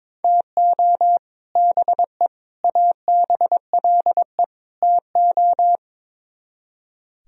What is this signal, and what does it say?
Morse code, heard as a keyed tone, about 22 wpm, 700 Hz. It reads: TO BE ABLE TO
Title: Speed-Racing - Sets of 4 Words 22wpm